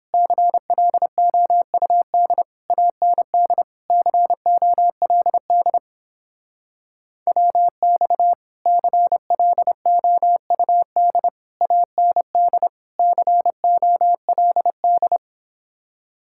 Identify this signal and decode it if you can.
Morse code, heard as a keyed tone, about 31 wpm, 700 Hz. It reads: CLOUD AND COLD WX CLOUD AND COLD